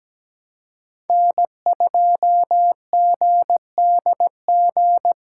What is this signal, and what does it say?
Morse code, heard as a keyed tone, about 20 wpm, 700 Hz. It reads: N2GDG
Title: Speed Racing - USA 1x3 Call Signs 17wpm